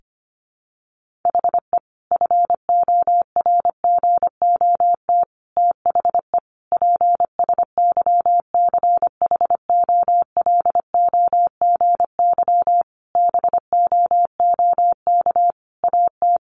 Morse code, heard as a keyed tone, about 25 wpm, 700 Hz. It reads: HE FORGOT THE PSYCHOLOGY BOOK AT
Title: Sentences from Top 1400 Words 25wpm